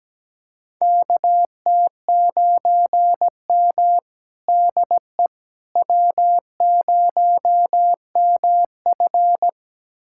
Morse code, heard as a keyed tone, about 17 wpm, 700 Hz. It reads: KT9M DE W0MF
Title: USA Callsign DE Callsign - 4 Characters - 17wpm